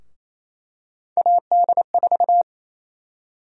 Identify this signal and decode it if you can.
Morse code, heard as a keyed tone, about 28 wpm, 700 Hz. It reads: AD4
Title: USA Call Sign Prefixes - 28wpm